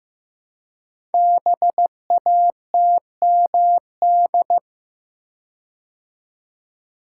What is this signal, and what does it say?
Morse code, heard as a keyed tone, about 15 wpm, 700 Hz. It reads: BATMD
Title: Code Groups - 5 Letters - 15wpm